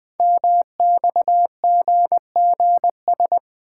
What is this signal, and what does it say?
Morse code, heard as a keyed tone, about 20 wpm, 700 Hz. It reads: MXGGS
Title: Code Groups - 5 Letters - 20wpm